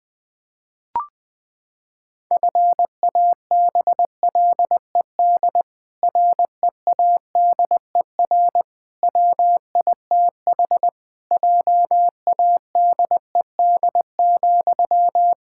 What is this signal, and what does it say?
Morse code, heard as a keyed tone, about 20 wpm, 700 Hz. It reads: FABLED READER WITH JADED,
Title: Pangram Challenge v2 - 20wpm